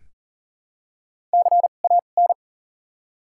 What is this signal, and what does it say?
Morse code, heard as a keyed tone, about 40 wpm, spaced as for 29 wpm, 700 Hz. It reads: CAN
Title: 2x Character Spacing - Top 200 Words - 40wpm